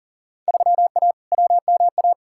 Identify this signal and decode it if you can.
Morse code, heard as a keyed tone, about 40 wpm, 700 Hz. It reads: 3A WMA
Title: Field Day Exchanges - 40wpm